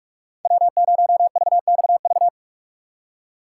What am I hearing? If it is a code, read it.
Morse code, heard as a keyed tone, about 46 wpm, 700 Hz. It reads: W0VXV
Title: USA Call Signs - 5 Characters - 4x Character Spacing - 45wpm